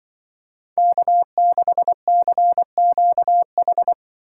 Morse code, heard as a keyed tone, about 24 wpm, 700 Hz. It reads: K6CQH
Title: Speed Racing - USA 1x3 Call Signs 20wpm